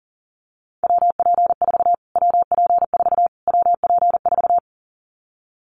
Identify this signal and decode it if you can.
Morse code, heard as a keyed tone, about 40 wpm, 700 Hz. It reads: WP4 WP4 WP4
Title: Callsign Prefixes for Top 25 DXCC - Repeated 3x - 40wpm